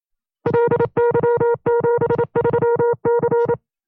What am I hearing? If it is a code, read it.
Morse code, heard as a keyed tone, about 27 wpm, 470 Hz. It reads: LY73C
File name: 25WPM